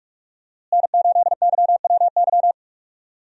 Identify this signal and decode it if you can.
Morse code, heard as a keyed tone, about 41 wpm, 700 Hz. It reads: N8YWY
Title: USA Call Signs - 5 Characters - Repeated 3x - 45wpm